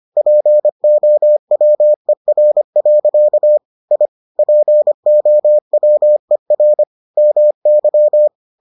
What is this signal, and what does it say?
Morse code, heard as a keyed tone, about 25 wpm, 600 Hz. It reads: POWER. I POWER MY